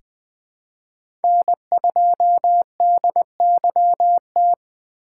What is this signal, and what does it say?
Morse code, heard as a keyed tone, about 20 wpm, 700 Hz. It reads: N2DYT
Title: K1USN SST QSO Practice - 20wpm